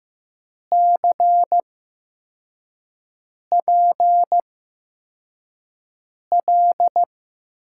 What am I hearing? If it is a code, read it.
Morse code, heard as a keyed tone, about 15 wpm, spaced as for 4 wpm, 700 Hz. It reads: CPL
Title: USA Call Signs - 8x Character Spacing - 15wpm